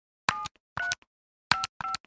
DTMF, heard as telephone touch tones, tones of about 160 ms. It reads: 7355
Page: Dialling buttons on touch tone cell phone.